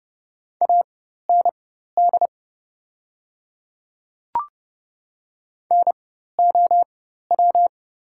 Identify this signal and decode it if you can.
Morse code, heard as a keyed tone, about 30 wpm, spaced as for 14 wpm, 700 Hz. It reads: AND NOW
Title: 4x Character Spacing - Top 100 Words - 30wpm